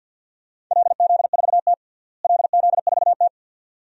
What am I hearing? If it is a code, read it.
Morse code, heard as a keyed tone, about 52 wpm, 700 Hz. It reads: LZ4T LZ4T
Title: DXCC-100 Call Signs - 4 Characters - Repeated 2x - 50wpm